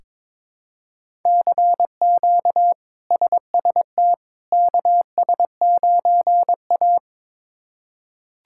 Morse code, heard as a keyed tone, about 21 wpm, 700 Hz. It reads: CQ SST KS9A
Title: K1USN SST QSO Practice - 22wpm